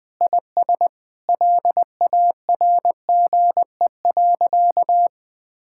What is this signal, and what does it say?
Morse code, heard as a keyed tone, about 21 wpm, 700 Hz. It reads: IS LARGE.
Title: Sentences with 4-Word Phrases - 20wpm